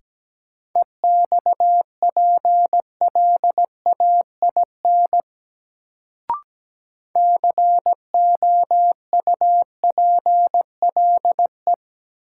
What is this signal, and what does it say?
Morse code, heard as a keyed tone, about 17 wpm, 700 Hz. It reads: EXPLAIN COUPLE
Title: Sentences from Top 1100 Words - New Word Drill - 17wpm